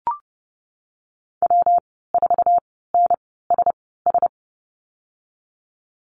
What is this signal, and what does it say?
Morse code, heard as a keyed tone, about 30 wpm, spaced as for 17 wpm, 700 Hz. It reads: W4NSS
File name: Lesson 19g - Letter w - Call Signs 30wpm